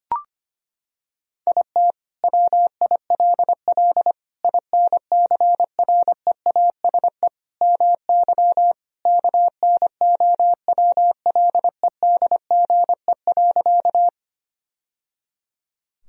Morse code, heard as a keyed tone, about 25 wpm, 700 Hz. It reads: IT WILL INCREASE MY KNOWLEDGE.
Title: Sentences from Top 1000 Words - Review Part 4 - 25wpm